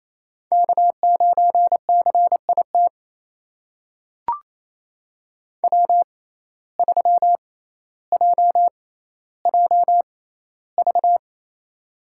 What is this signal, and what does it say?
Morse code, heard as a keyed tone, about 28 wpm, spaced as for 14 wpm, 700 Hz. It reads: K9CIT W3JJV
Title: USA Call Signs - 5 Characters - 6x Character Spacing - 28wpm